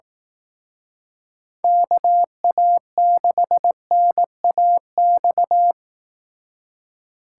Morse code, heard as a keyed tone, about 18 wpm, 700 Hz. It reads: KA6NAX
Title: Speed Racing - USA 2x3 Call Signs 15wpm